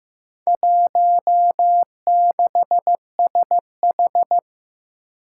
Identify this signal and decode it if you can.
Morse code, heard as a keyed tone, about 15 wpm, 700 Hz. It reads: 16SH
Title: Four Challenging Letter-Number Combinations 15wpm